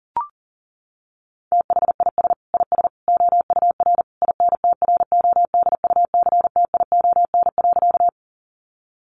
Morse code, heard as a keyed tone, about 37 wpm, 700 Hz. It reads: THIS IS OUR INTRODUCTION.
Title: Sentences from Top 1000 Words 40wpm